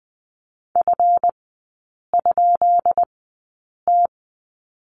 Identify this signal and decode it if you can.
Morse code, heard as a keyed tone, about 20 wpm, spaced as for 8 wpm, 700 Hz. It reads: F?T
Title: 138 Lesson 23e - Letter f - Three Characters 20wpm